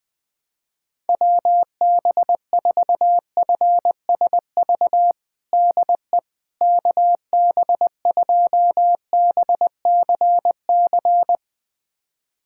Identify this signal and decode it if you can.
Morse code, heard as a keyed tone, about 20 wpm, 700 Hz. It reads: WB4FSV DE KB2BCC
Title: USA Callsign DE Callsign - 6 Characters - 20wpm